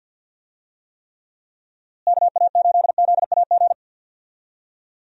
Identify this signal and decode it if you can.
Morse code, heard as a keyed tone, about 46 wpm, 700 Hz. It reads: KA8ZAG
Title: USA Call Signs - 6 Characters - Repeated 2x - 50wpm